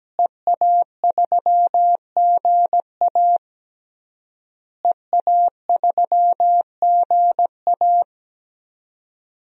Morse code, heard as a keyed tone, about 17 wpm, 700 Hz. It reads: EA3GA EA3GA
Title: DXCC-100 Call Signs - 5 Characters - Repeated 3x - 17wpm